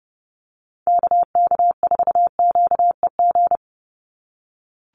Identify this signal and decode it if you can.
Morse code, heard as a keyed tone, about 30 wpm, 700 Hz. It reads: KK4QEG
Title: USA Call Signs 30wpm